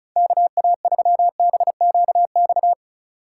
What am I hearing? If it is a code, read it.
Morse code, heard as a keyed tone, about 44 wpm, 700 Hz. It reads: KA3BQX